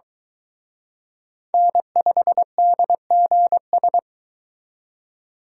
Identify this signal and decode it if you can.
Morse code, heard as a keyed tone, about 23 wpm, 700 Hz. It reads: N5DGS